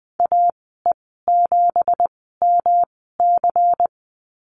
Morse code, heard as a keyed tone, about 20 wpm, spaced as for 15 wpm, 700 Hz. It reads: AE7MC
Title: 238 Lesson 38g - Letter q - Call Signs 20wpm